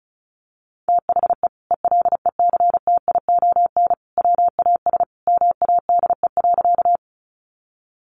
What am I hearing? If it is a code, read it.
Morse code, heard as a keyed tone, about 34 wpm, 700 Hz. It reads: THE ELECTION WAS MADE.
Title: Sentences from Top 1500 Words 35wpm